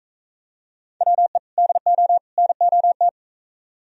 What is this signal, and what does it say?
Morse code, heard as a keyed tone, about 42 wpm, 700 Hz. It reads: WE DO NOT